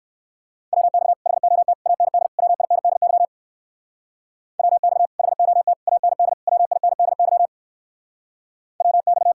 Morse code, heard as a keyed tone, about 57 wpm, 700 Hz. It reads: WX HOT AND WINDY WX HOT AND WINDY WX